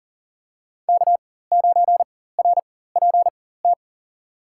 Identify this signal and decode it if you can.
Morse code, heard as a keyed tone, about 37 wpm, spaced as for 18 wpm, 700 Hz. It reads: K9RPT
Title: USA Call Signs - 5 Characters - 4x Character Spacing - 40wpm